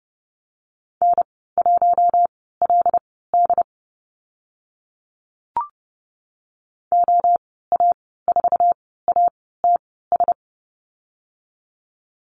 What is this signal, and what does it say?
Morse code, heard as a keyed tone, about 30 wpm, spaced as for 17 wpm, 700 Hz. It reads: N1LD OA4ATS
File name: Lesson 14g - Number 2 - Call Signs 30wpm